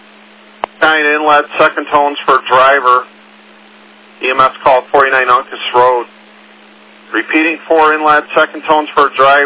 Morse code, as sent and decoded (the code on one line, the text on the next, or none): none